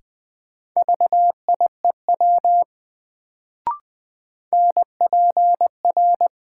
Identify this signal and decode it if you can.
Morse code, heard as a keyed tone, about 20 wpm, 700 Hz. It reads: VIEW NPR
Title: Sets of 5 Words - New Word Drill - 20wpm